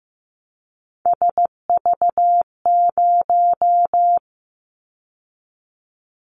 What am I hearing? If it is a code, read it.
Morse code, heard as a keyed tone, about 15 wpm, 700 Hz. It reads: SV0
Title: Callsign Prefixes for Top 50 DXCC - 15wpm